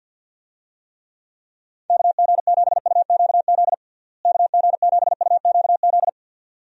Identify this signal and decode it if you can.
Morse code, heard as a keyed tone, about 50 wpm, 700 Hz. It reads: KG7UQZ KG7UQZ
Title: USA Call Signs - 6 Characters - Repeated 2x - 50wpm